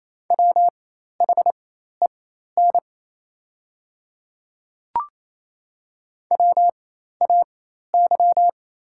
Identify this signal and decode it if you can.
Morse code, heard as a keyed tone, about 28 wpm, spaced as for 13 wpm, 700 Hz. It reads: WHEN WAY
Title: 4x Character Spacing - Top 100 Words - 28wpm